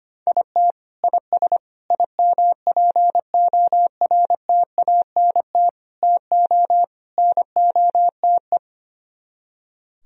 Morse code, heard as a keyed tone, about 24 wpm, 700 Hz. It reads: IT IS IMPORTANT TO NOTE
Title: Speed-Racing - Sets of 5 Words 25wpm